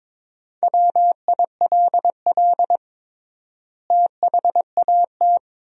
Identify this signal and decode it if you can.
Morse code, heard as a keyed tone, about 22 wpm, 700 Hz. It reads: WILL THAT
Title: 3x Word-Spacing - Sentences from Top 100 Words - 22wpm